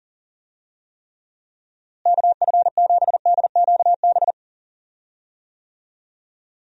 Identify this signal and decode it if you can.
Morse code, heard as a keyed tone, about 40 wpm, 700 Hz. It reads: KF7DQB
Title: USA Call Signs - 6 Characters - Repeated 2x - 40wpm